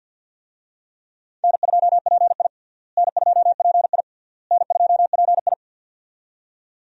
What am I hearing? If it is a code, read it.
Morse code, heard as a keyed tone, about 51 wpm, 700 Hz. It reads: N2PI N2PI N2PI
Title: USA Call Signs - 4 Characters - Repeated 3x - 50wpm